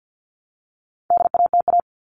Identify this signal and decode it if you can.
Morse code, heard as a keyed tone, about 50 wpm, 700 Hz. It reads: DATA
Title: Top 501-750 Words 50wpm